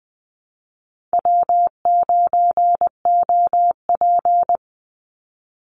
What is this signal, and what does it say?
Morse code, heard as a keyed tone, about 20 wpm, 700 Hz. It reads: W9OP
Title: Call Signs Repeated 3x - 20wpm